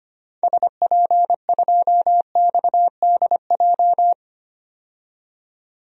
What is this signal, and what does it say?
Morse code, heard as a keyed tone, about 25 wpm, 690 Hz. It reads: SP2XDJ
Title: DXCC-100 Call Signs - 6 Characters - 25wpm